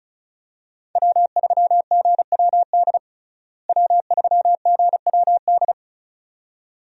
Morse code, heard as a keyed tone, about 36 wpm, 700 Hz. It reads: W3GWD W3GWD
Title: USA Call Signs - 5 Characters - Repeated 2x - 35wpm